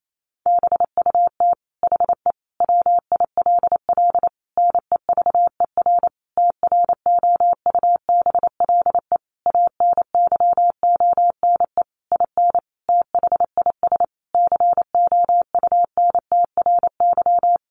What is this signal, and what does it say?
Morse code, heard as a keyed tone, about 28 wpm, 700 Hz. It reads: BUT HE WILL NEVER TROUBLE ANYONE IN THIS COUNTRY
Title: Sentences from Top 1200 Words 28wpm